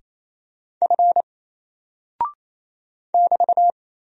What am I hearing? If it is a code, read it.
Morse code, heard as a keyed tone, about 27 wpm, 700 Hz. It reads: F <BT>
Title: Single Character (Letter:Number:Punctuation:Prosign) 28wpm